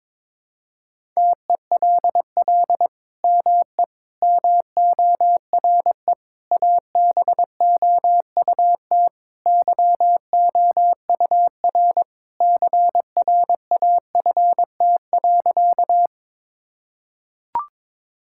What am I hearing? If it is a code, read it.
Morse code, heard as a keyed tone, about 22 wpm, 700 Hz. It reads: TELL ME MORE ABOUT YOUR CRAFT.
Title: Sentences from Top 1000 Words - Review Part 6 - 22wpm